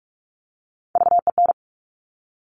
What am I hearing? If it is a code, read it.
Morse code, heard as a keyed tone, about 46 wpm, 700 Hz. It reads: VEN
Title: Top Three Letter Combinations in Words 45wpm